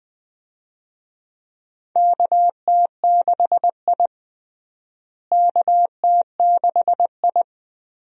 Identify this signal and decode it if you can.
Morse code, heard as a keyed tone, about 20 wpm, 700 Hz. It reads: KT6I KT6I
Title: USA Call Signs - 4 Characters - Repeated 3x - 20wpm